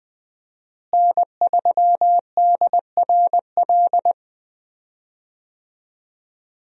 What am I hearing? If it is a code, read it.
Morse code, heard as a keyed tone, about 20 wpm, 700 Hz. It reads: N3DRL
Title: USA Call Signs - 5 Characters - Repeated 2x - 20wpm